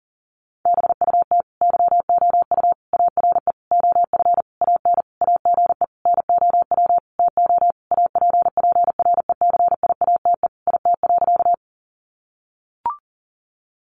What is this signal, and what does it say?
Morse code, heard as a keyed tone, about 40 wpm, 700 Hz. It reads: BUT YOU ARE OF AN AGE NOW TO APPRECIATE IT.
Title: Sentences from Top 1300 Words 40wpm